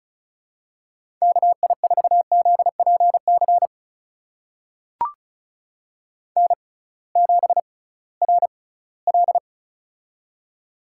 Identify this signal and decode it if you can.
Morse code, heard as a keyed tone, about 36 wpm, spaced as for 20 wpm, 700 Hz. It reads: KI4ZPC N7RL
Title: USA Call Signs - 6x Character Spacing - 35wpm